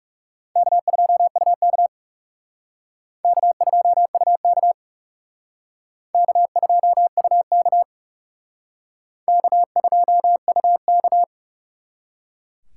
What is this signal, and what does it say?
Morse code, heard as a keyed tone, about 38 wpm, 700 Hz. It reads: K2UK K2UK K2UK K2UK